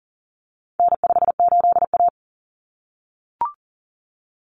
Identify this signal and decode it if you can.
Morse code, heard as a keyed tone, about 40 wpm, 700 Hz. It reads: N58A
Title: Common Cut Numbers - 4-Digit 40wpm